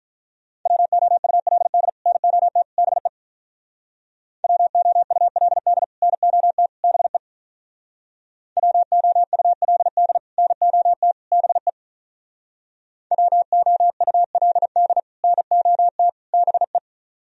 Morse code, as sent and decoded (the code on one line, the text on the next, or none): .-- --- ..- .-.. -.. / -. --- - / -... / .-- --- ..- .-.. -.. / -. --- - / -... / .-- --- ..- .-.. -.. / -. --- - / -... / .-- --- ..- .-.. -.. / -. --- - / -... .
WOULD NOT BE WOULD NOT BE WOULD NOT BE WOULD NOT BE